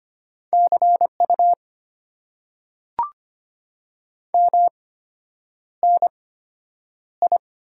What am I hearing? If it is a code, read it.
Morse code, heard as a keyed tone, about 25 wpm, spaced as for 9 wpm, 690 Hz. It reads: CU MNI
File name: Top 90 QSO Elements - 8x Character Spacing - 25wpm